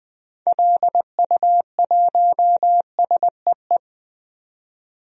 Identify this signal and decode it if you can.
Morse code, heard as a keyed tone, about 20 wpm, 700 Hz. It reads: LU1SEE
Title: DXCC-100 Call Signs - 6 Characters - 20wpm